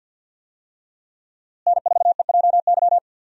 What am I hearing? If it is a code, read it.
Morse code, heard as a keyed tone, about 52 wpm, 700 Hz. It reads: N4EJY
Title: USA Call Signs - 5 Characters - Repeated 3x - 50wpm